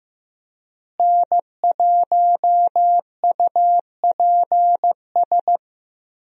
Morse code, heard as a keyed tone, about 15 wpm, 700 Hz. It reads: N1UPS
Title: USA Call Signs - 5 Characters - Repeated 3x - 15wpm